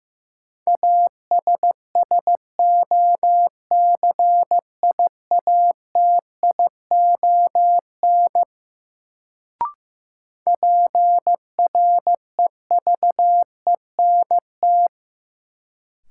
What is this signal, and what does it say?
Morse code, heard as a keyed tone, about 15 wpm, 700 Hz. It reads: ASSOCIATION PREVENT
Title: Sentences from Top 1100 Words - New Word Drill - 15wpm